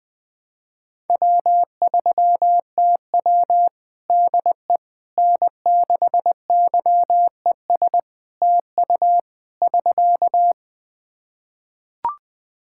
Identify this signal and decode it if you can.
Morse code, heard as a keyed tone, about 20 wpm, 700 Hz. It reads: W3TW DE N6YES TU <SK>
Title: Practice QSO - Exchange 4 - Ending - 20wpm